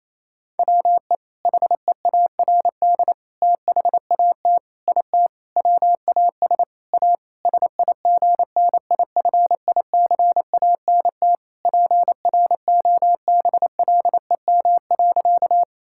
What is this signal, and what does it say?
Morse code, heard as a keyed tone, about 28 wpm, 700 Hz. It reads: WE HEARD THAT IT WAS A SIGNIFICANT PROBLEM.
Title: Sentences from Top 1000 Words - Review Part 6 - 28wpm